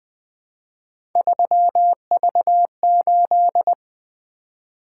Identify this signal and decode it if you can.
Morse code, heard as a keyed tone, about 20 wpm, 700 Hz. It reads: 3V8
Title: Three Letter-Number Combinations - 20wpm